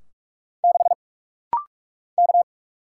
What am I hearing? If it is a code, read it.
Morse code, heard as a keyed tone, about 44 wpm, 700 Hz. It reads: 6 K
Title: Single Letter-Number - Rapid-Fire 45wpm